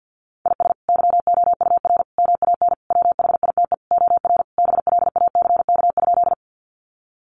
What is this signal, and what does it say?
Morse code, heard as a keyed tone, about 50 wpm, 700 Hz. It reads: IS YOUR MAN WHITE OR BLACK?
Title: Sentences from Top 400 Words 50wpm